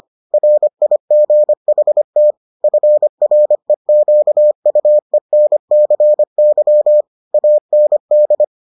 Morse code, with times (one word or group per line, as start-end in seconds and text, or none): 0.3-2.3 RIGHT
2.6-7.0 FREQUENCY
7.3-8.4 AND